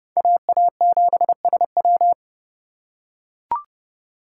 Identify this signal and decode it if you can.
Morse code, heard as a keyed tone, about 30 wpm, 700 Hz. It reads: AA7SW
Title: USA Call Signs - 4x Character Spacing - 30wpm